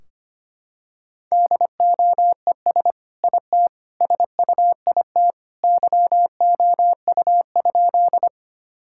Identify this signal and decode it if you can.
Morse code, heard as a keyed tone, about 25 wpm, 700 Hz. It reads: DOES IT SUIT YOU?
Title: Sentences from Top 1600 Words 25wpm